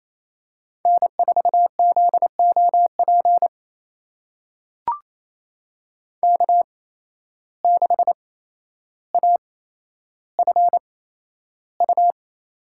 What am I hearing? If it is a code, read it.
Morse code, heard as a keyed tone, about 28 wpm, spaced as for 12 wpm, 700 Hz. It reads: N4ZOP K6AFU
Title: USA Call Signs - 5 Characters - 8x Character Spacing - 28wpm